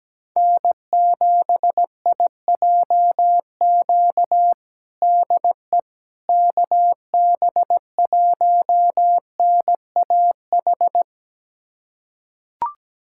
Morse code, heard as a keyed tone, about 17 wpm, 700 Hz. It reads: N7IJQ DE KB1NAH
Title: USA Callsign DE Callsign 17wpm